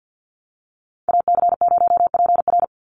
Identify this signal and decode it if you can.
Morse code, heard as a keyed tone, about 49 wpm, 700 Hz. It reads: AC0PR
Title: NA SOTA Chaser Callsigns 50wpm